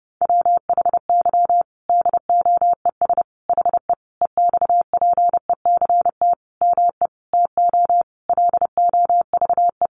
Morse code, heard as a keyed tone, about 28 wpm, 700 Hz. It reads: WHY DOES HE EXPECT ME TO LOVE
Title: Sentences from Top 1100 Words 30wpm